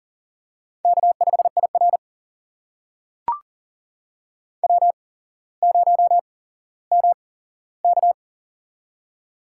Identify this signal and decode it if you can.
Morse code, heard as a keyed tone, about 43 wpm, spaced as for 17 wpm, 700 Hz. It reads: K5IR W0MK
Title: USA Call Signs - 4 Characters - 8x Character Spacing - 40wpm